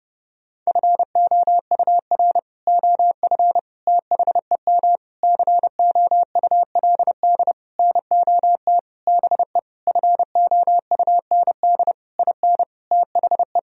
Morse code, heard as a keyed tone, about 32 wpm, 700 Hz. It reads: FOUR OF THEM COULD NOT BE FOUND IN THE
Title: Sentences from Top 1000 Words - Review Part 1 - 30wpm